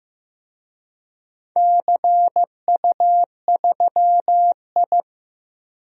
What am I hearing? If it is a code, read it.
Morse code, heard as a keyed tone, about 15 wpm, 700 Hz. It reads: CU3I
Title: DXCC-100 Call Signs - 4 Characters - 15wpm